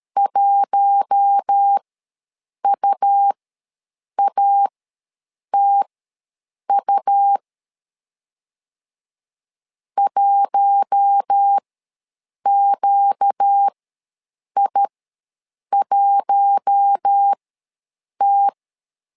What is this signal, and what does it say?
Morse code, heard as a keyed tone, about 13 wpm, spaced as for 7 wpm, 800 Hz. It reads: 1UATU 1QI1T
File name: LEZIONI / 4 Q1 ws 12wpm 2129